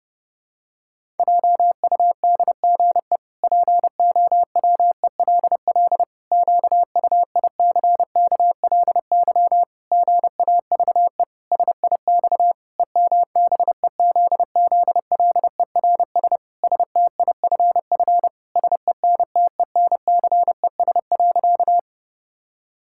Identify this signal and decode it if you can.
Morse code, heard as a keyed tone, about 30 wpm, 700 Hz. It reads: JUDGE POWELL QUICKLY GAVE SIX EMBEZZLERS STIFF SENTENCES.
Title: Pangram Challenge v2 - 30wpm